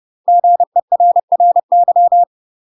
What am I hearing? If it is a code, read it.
Morse code, heard as a keyed tone, about 30 wpm, 700 Hz. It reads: GERRY
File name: UK MASTER QSOs TRAINING / 30WPM / UK MASTER QSOs 0028